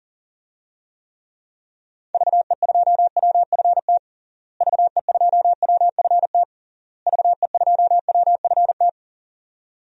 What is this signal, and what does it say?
Morse code, heard as a keyed tone, about 40 wpm, 700 Hz. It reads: VE2WFT VE2WFT VE2WFT
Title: DXCC-100 Call Signs - 6 Characters - Repeated 3x - 40wpm